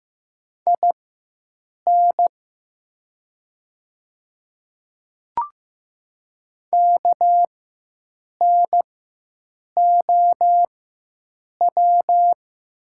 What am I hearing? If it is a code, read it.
Morse code, heard as a keyed tone, about 15 wpm, spaced as for 7 wpm, 700 Hz. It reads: IN KNOW
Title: 4x Character Spacing - Top 100 Words - 15wpm